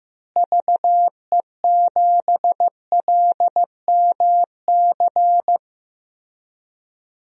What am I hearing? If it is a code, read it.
Morse code, heard as a keyed tone, about 15 wpm, 700 Hz. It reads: VE7LMC
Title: DXCC-100 Call Signs - 6 Characters - Repeated 3x - 15wpm